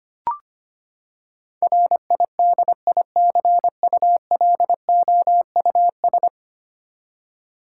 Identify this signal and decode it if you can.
Morse code, heard as a keyed tone, about 25 wpm, 700 Hz. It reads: RIDICULOUS
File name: practice / 25wpm / Ten Letter Words - 25wpm